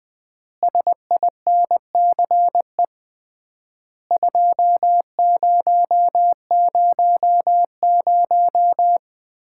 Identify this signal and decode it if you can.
Morse code, heard as a keyed tone, about 20 wpm, 700 Hz. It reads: SINCE 2000
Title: Common QSO Phrases - 3x Word-Spacing - 20wpm